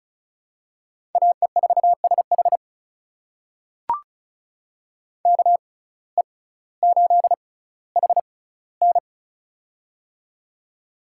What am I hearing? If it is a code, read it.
Morse code, heard as a keyed tone, about 36 wpm, spaced as for 18 wpm, 700 Hz. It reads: AE4SH KE8HN
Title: USA Call Signs - 5 Characters - 6x Character Spacing - 35wpm